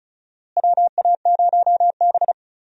Morse code, heard as a keyed tone, about 34 wpm, 700 Hz. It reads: WA0B
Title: USA Call Signs - 4 Characters - Repeated 3x - 35wpm